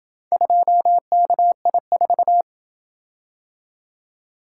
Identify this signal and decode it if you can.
Morse code, heard as a keyed tone, about 27 wpm, 700 Hz. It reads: 2KI4